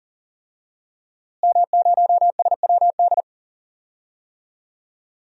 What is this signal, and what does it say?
Morse code, heard as a keyed tone, about 40 wpm, 700 Hz. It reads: M0SWD
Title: DXCC-100 Call Signs - 5 Characters - Repeated 2x - 40wpm